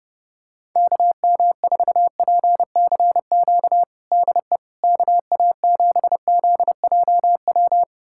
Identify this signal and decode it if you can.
Morse code, heard as a keyed tone, about 30 wpm, 700 Hz. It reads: KM4PCQ DE KA7ZJW
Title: USA Callsign DE Callsign - 6 Characters - 30wpm